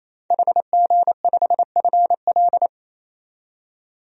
Morse code, heard as a keyed tone, about 27 wpm, 700 Hz. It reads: HG5FL
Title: DXCC-100 Call Signs - 5 Characters - 28wpm